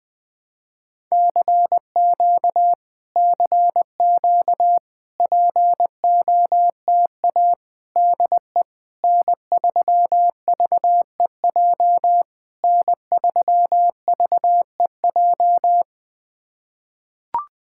Morse code, heard as a keyed tone, about 20 wpm, 700 Hz. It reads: CQ CQ POTA DE N3VEJ N3VEJ
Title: POTA QSO Practice - 20wpm